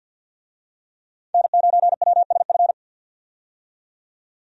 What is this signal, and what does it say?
Morse code, heard as a keyed tone, about 52 wpm, 700 Hz. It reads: N9WSF
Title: USA Call Signs - 5 Characters - 50wpm